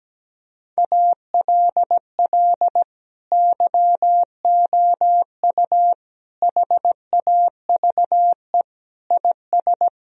Morse code, heard as a keyed tone, about 17 wpm, 700 Hz. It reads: ALL YOU HAVE IS
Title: Sentences from Top 1000 Words - Review Part 3 - 17wpm